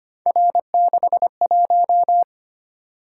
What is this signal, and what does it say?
Morse code, heard as a keyed tone, about 25 wpm, 700 Hz. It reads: R61
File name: Three Letter-Number Combinations - 25wpm